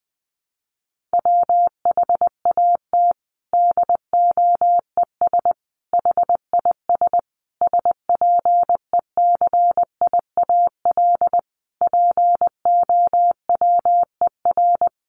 Morse code, heard as a keyed tone, about 20 wpm, 700 Hz. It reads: WHAT DOES HIS SPECIAL POWER
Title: Sentences from Top 1400 Words 20wpm